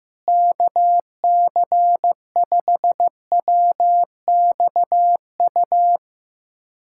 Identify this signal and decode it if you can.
Morse code, heard as a keyed tone, about 15 wpm, 700 Hz. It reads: KC5WXU
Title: USA Callsign DE Callsign - 6 Characters - 15wpm